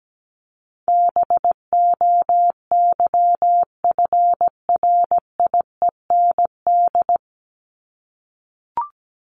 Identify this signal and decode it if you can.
Morse code, heard as a keyed tone, about 17 wpm, 700 Hz. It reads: BOYFRIEND
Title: Top 501-750 Words 17wpm